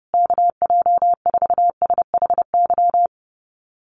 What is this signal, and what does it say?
Morse code, heard as a keyed tone, about 30 wpm, 700 Hz. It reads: KJ4SHY
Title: USA Call Signs Repeated 3x 30wpm